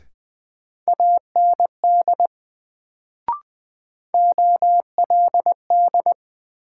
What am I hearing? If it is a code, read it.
Morse code, heard as a keyed tone, about 20 wpm, 700 Hz. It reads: AND OLD
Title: Sets of 2 Words - New Word Drill - 20wpm